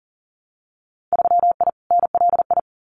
Morse code, heard as a keyed tone, about 42 wpm, 700 Hz. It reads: 3I NLI
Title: Winter Field Day Exchanges 40wpm